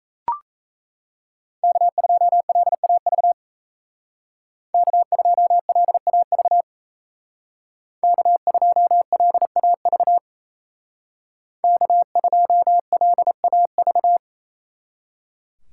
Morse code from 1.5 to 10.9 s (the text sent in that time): K2LAV K2LAV K2LAV